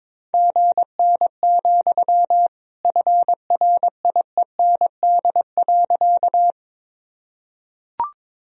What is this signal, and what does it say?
Morse code, heard as a keyed tone, about 22 wpm, 700 Hz. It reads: GN, FRIEND.
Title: Sentences from Top 300 Words and Abbreviations 22wpm